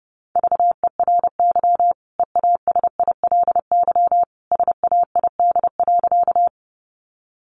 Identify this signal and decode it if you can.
Morse code, heard as a keyed tone, about 30 wpm, 700 Hz. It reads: VERY EASILY SAID.
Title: Sentences from Top 900 Words 30wpm